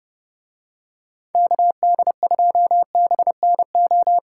Morse code, heard as a keyed tone, about 27 wpm, 700 Hz. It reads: KD2BNO